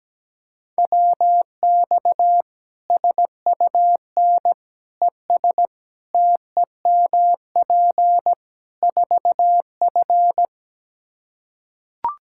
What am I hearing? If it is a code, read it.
Morse code, heard as a keyed tone, about 17 wpm, 700 Hz. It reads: WX SUN ES TEMP 4F
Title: Practice QSO - Exchange 2 - Info - 17wpm